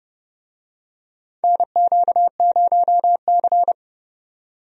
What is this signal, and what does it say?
Morse code, heard as a keyed tone, about 30 wpm, 700 Hz. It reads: NQ0C
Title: USA Call Signs - 4 Characters - 30wpm